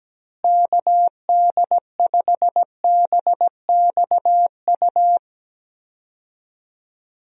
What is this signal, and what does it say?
Morse code, heard as a keyed tone, about 17 wpm, 700 Hz. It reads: KD5BXU
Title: USA Call Signs - 6 Characters - Repeated 3x - 17wpm